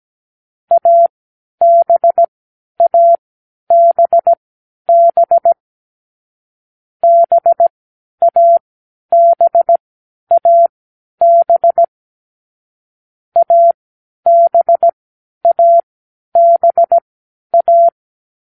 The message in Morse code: ABABB BABAB ABABA